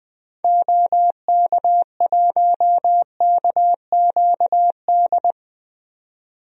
Morse code, heard as a keyed tone, about 20 wpm, 700 Hz. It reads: OK1KQD
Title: DXCC-100 Call Signs - 6 Characters - 20wpm